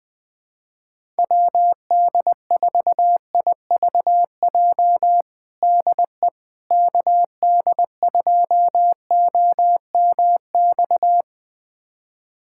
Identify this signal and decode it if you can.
Morse code, heard as a keyed tone, about 20 wpm, 700 Hz. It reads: WD4IVJ DE KD2OMX
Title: USA Callsign DE Callsign 20wpm